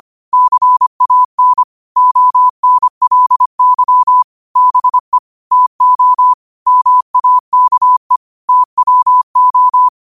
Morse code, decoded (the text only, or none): CAN ONLY BE TO MAKE TWO